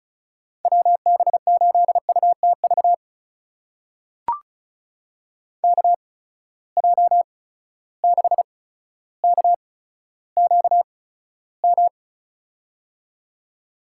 Morse code, heard as a keyed tone, about 36 wpm, spaced as for 15 wpm, 700 Hz. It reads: WB8UTV KJ6KQM
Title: USA Call Signs - 6 Characters - 8x Character Spacing - 35wpm